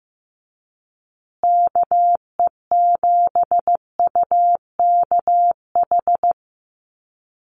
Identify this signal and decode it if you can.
Morse code, heard as a keyed tone, about 15 wpm, 700 Hz. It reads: KE7UKH
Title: USA Call Signs Repeated 3x 15wpm